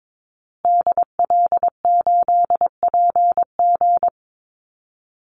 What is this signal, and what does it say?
Morse code, heard as a keyed tone, about 22 wpm, 690 Hz. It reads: DL8PG
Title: Call Signs Repeated 2x - 22wpm